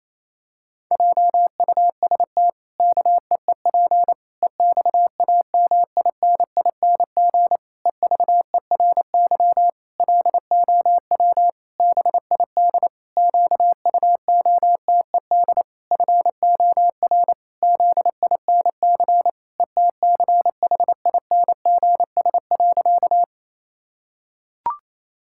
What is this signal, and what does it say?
Morse code, heard as a keyed tone, about 28 wpm, 700 Hz. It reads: JUST KEEP EXAMINING EVERY LOW BID QUOTED FOR ZINC ETCHINGS.
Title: Pangram Challenge v2 - 28wpm